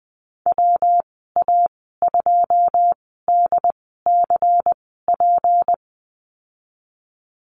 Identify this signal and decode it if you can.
Morse code, heard as a keyed tone, about 20 wpm, spaced as for 14 wpm, 700 Hz. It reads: WA2DCP
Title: 179 Lesson 29g - Slash - Call Signs 20wpm